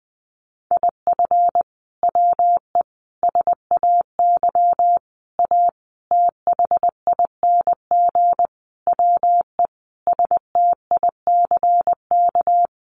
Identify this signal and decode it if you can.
Morse code, heard as a keyed tone, about 20 wpm, 700 Hz. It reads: IF WE SAY A THING WE STICK
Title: Sentences from Top 1200 Words 20wpm